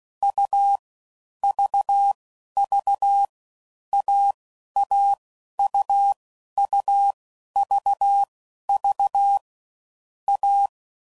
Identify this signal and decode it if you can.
Morse code, heard as a keyed tone, about 16 wpm, 800 Hz. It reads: U V V A A U U V V A
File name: Combinations / Difficult Combos / AUV AUV4